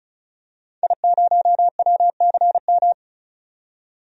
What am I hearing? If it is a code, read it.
Morse code, heard as a keyed tone, about 34 wpm, 700 Hz. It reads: I0WCM
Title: DXCC-100 Call Signs - 5 Characters - 35wpm